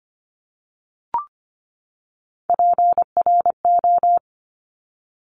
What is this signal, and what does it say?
Morse code, heard as a keyed tone, about 25 wpm, 700 Hz. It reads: PRO
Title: Top Three Letter Combinations in Words 25wpm